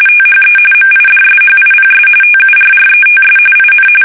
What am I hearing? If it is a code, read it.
RTTY, about 45.45 baud. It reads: THE MIDSHIPS TURRETS IN
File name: fsk850 / battleships